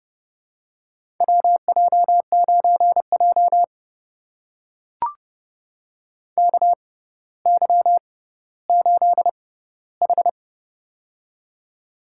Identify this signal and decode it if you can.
Morse code, heard as a keyed tone, about 30 wpm, spaced as for 15 wpm, 700 Hz. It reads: WJ9J KY8H
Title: USA Call Signs - 4 Characters - 6x Character Spacing - 30wpm